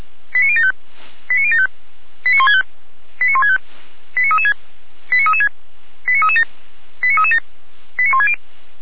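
Five-tone selective call (ZVEI), heard as quick runs of five tones, tones of about 70 ms. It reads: 89075 89075 89165 89165 89336 89336 89337 89337 89160
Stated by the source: ZVEI five-tone call 89123 received by SDR, feuerwehr-orlamuende.org fire brigade audio archive.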